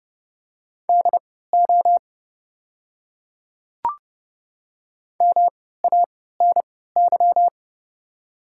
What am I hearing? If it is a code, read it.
Morse code, heard as a keyed tone, about 30 wpm, spaced as for 17 wpm, 700 Hz. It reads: DO MANY
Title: 3x Character Spacing - Top 300 Words - 30wpm